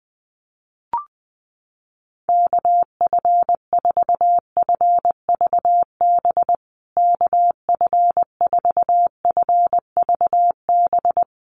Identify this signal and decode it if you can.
Morse code, heard as a keyed tone, about 20 wpm, 700 Hz. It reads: KF4FVB KF4FVB
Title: USA Call Signs Repeated 3x 20wpm